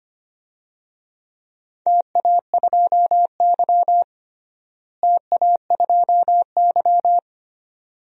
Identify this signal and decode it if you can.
Morse code, heard as a keyed tone, about 25 wpm, 700 Hz. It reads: TA2Y TA2Y
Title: DXCC-100 Call Signs - 4 Characters - Repeated 2x - 25wpm